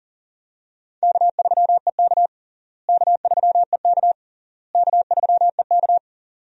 Morse code, heard as a keyed tone, about 37 wpm, 700 Hz. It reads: K3EK K3EK K3EK
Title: USA Call Signs - 4 Characters - Repeated 3x - 40wpm